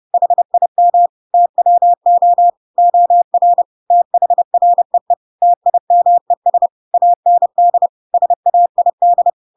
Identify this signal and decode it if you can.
Morse code, heard as a keyed tone, about 30 wpm, 700 Hz. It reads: HIM TWO OR THREE TIMES AND SAID
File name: Pinocchio / Pinocchio ch3 30 30 ik1hge